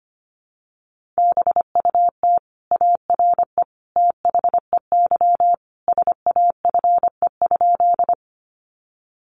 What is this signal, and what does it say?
Morse code, heard as a keyed tone, about 25 wpm, 700 Hz. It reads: BUT ARE THEY SAFE?
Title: Sentences from Top 900 Words 25wpm